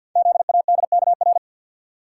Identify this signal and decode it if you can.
Morse code, heard as a keyed tone, about 50 wpm, 690 Hz. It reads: ZADKR
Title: Code Groups - 5 Letters - 50wpm